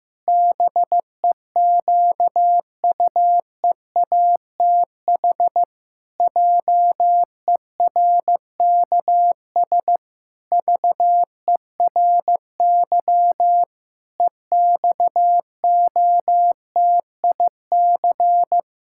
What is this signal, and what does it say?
Morse code, heard as a keyed tone, about 14 wpm, 700 Hz. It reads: BEQUEATH JERKS VERY EXOTIC
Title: Pangram Challenge v2 - 15wpm